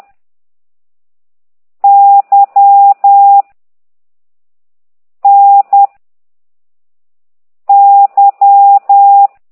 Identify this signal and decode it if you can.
Morse code, heard as a keyed tone, about 10 wpm, 800 Hz. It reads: Y N Y
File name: ct2idw / exerc4